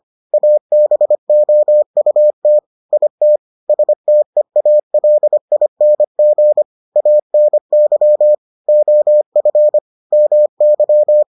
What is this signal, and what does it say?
Morse code, heard as a keyed tone, about 25 wpm, 600 Hz. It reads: ABOUT IT STEALING ANY OF MY